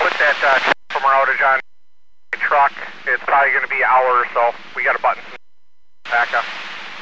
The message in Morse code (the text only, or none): none